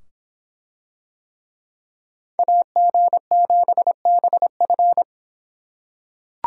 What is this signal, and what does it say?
Morse code, heard as a keyed tone, about 25 wpm, 700 Hz. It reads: AG7BF